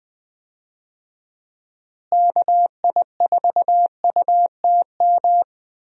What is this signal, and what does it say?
Morse code, heard as a keyed tone, about 20 wpm, 700 Hz. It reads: KI4UTM
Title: USA Call Signs - 6 Characters - Repeated 3x - 20wpm